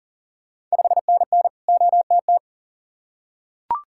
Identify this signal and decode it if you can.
Morse code, heard as a keyed tone, about 40 wpm, 700 Hz. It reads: 5NN OTT
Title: Ontario QSO Party Exchange 40wpm